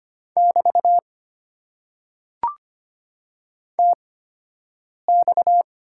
Morse code, heard as a keyed tone, about 25 wpm, spaced as for 7 wpm, 700 Hz. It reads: <BT> TX
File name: Top 90 QSO Elements - 8x Character Spacing - 25wpm